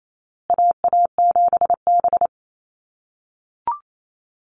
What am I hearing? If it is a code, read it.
Morse code, heard as a keyed tone, about 28 wpm, 700 Hz. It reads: AA7B
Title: USA Call Signs Repeated 2x 28wpm